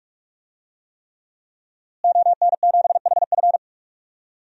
Morse code, heard as a keyed tone, about 46 wpm, 700 Hz. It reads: ON7HF
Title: DXCC-100 Call Signs - 5 Characters - Repeated 3x - 45wpm